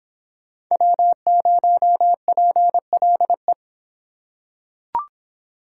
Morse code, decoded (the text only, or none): W0PLE